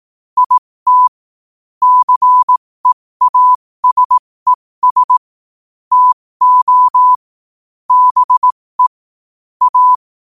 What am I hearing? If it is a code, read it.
Morse code, heard as a keyed tone, about 18 wpm, spaced as for 15 wpm, 1000 Hz. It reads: IT CEASES TO BE A
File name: qotdfiles / 15wpm